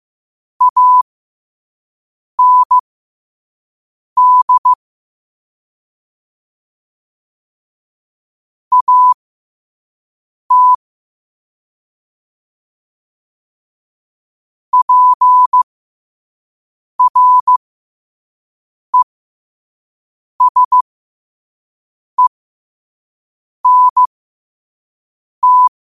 Morse code, heard as a keyed tone, about 15 wpm, spaced as for 5 wpm, 1000 Hz. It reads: AND AT PRESENT
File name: qotdfiles / 5wpm